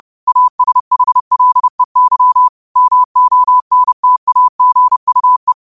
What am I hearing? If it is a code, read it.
Morse code, heard as a keyed tone, about 30 wpm, 1000 Hz. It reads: ASHLEY MONTAGUE
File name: qotdfiles / 30wpm